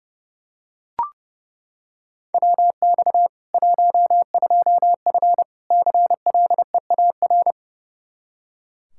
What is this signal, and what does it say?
Morse code, heard as a keyed tone, about 30 wpm, 700 Hz. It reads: WX 12F CLEAR
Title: Practice QSO - Exchange 2 - Info - 30wpm